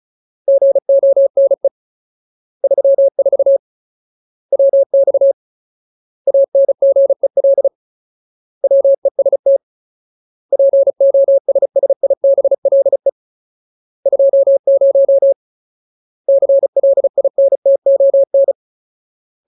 Morse code, heard as a keyed tone, about 34 wpm, 550 Hz. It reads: GONE 34 WX ANGEL WEST POSSIBLE 20 CLINTON
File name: Ultimate Word List / 35WPM Ultimate